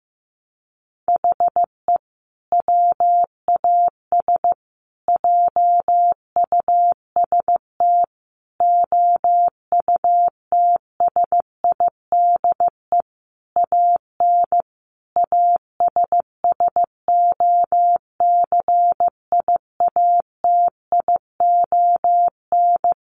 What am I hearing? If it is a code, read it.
Morse code, heard as a keyed tone, about 14 wpm, 700 Hz. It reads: HE WAS JUST OUTSIDE AN ASSOCIATION
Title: Sentences from Top 1100 Words 15wpm